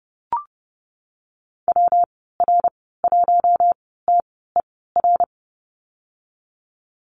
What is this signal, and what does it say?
Morse code, heard as a keyed tone, about 30 wpm, spaced as for 17 wpm, 700 Hz. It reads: WR1TER
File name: Lesson 19g - Letter w - Call Signs 30wpm